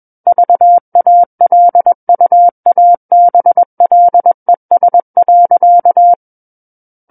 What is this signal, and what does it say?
Morse code, heard as a keyed tone, about 21 wpm, 700 Hz. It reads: VALUABLES.
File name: Book One / 12-Chapter 12 - 21 WPM